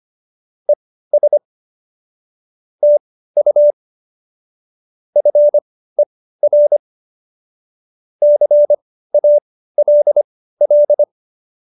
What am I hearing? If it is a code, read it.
Morse code, heard as a keyed tone, about 25 wpm, spaced as for 15 wpm, 590 Hz. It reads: ES TU FER CALL
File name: qso219b 15